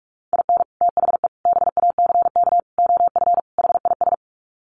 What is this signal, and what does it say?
Morse code, heard as a keyed tone, about 44 wpm, 700 Hz. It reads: IN THE BACK OF HIS
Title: Sets of 5 Words Encore2 45wpm